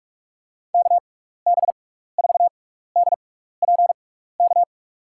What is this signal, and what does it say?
Morse code, heard as a keyed tone, about 44 wpm, spaced as for 15 wpm, 700 Hz. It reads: KB4DPK
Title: USA Call Signs - 6 Characters - 6x Character Spacing - 45wpm